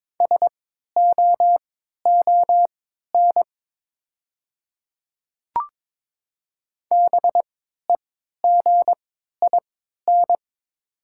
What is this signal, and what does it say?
Morse code, heard as a keyed tone, about 22 wpm, spaced as for 13 wpm, 700 Hz. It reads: SOON BEGIN
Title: 3x Character Spacing - Top 500 Words - 22wpm